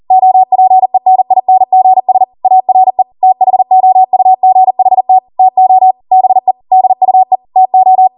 Morse code, heard as a keyed tone, about 40 wpm, 750 Hz. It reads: OPENINGS ARE THOUGHT TO BE DUE TO